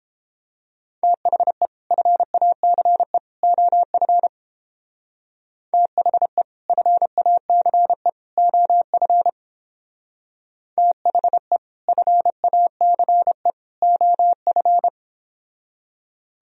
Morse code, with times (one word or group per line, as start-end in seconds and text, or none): 1.0-1.7 THE
1.9-3.2 FACE
3.4-4.3 OF
5.7-6.4 THE
6.7-8.1 FACE
8.4-9.3 OF
10.8-11.6 THE
11.9-13.5 FACE
13.8-14.9 OF